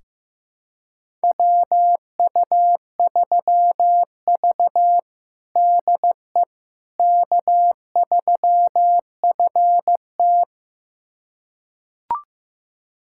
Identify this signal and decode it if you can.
Morse code, heard as a keyed tone, about 15 wpm, 700 Hz. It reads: WU3V DE K3FT
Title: USA Callsign DE Callsign - 4 Characters - 15wpm